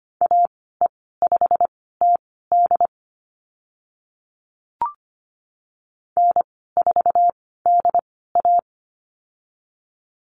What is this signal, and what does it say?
Morse code, heard as a keyed tone, about 25 wpm, spaced as for 16 wpm, 700 Hz. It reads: AE5TD N4DA
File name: Lesson 15g - Number 5 - Call Signs 25wpm